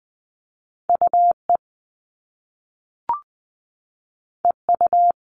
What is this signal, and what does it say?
Morse code, heard as a keyed tone, about 20 wpm, 700 Hz. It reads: UE EU
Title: All Cut Numbers - 2-Digit 20wpm